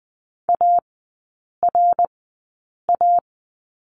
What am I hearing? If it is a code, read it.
Morse code, heard as a keyed tone, about 20 wpm, spaced as for 8 wpm, 700 Hz. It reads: ARA
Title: 093 Lesson 16e - Letter c - Three Characters 20wpm